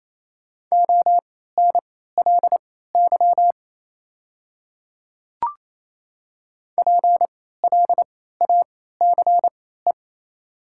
Morse code, heard as a keyed tone, about 27 wpm, spaced as for 16 wpm, 690 Hz. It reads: ONLY PLACE
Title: 3x Character Spacing - Top 200 Words - 28wpm